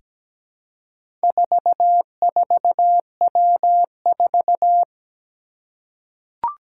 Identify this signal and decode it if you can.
Morse code, heard as a keyed tone, about 17 wpm, 700 Hz. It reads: 44W4
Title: Four Challenging Letter-Number Combinations 17wpm